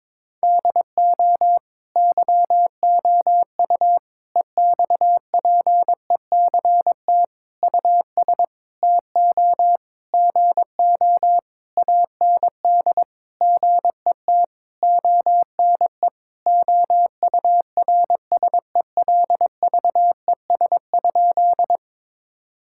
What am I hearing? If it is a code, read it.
Morse code, heard as a keyed tone, about 22 wpm, 700 Hz. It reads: DO YOU EXPECT US TO GO AND GET ONE OURSELVES?
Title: Sentences from Top 1600 Words 22wpm